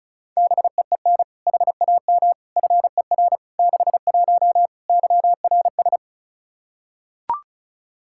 Morse code, heard as a keyed tone, about 34 wpm, 700 Hz. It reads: BEEN HAM FER 61 YRS
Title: K1USN SST QSO Practice 35wpm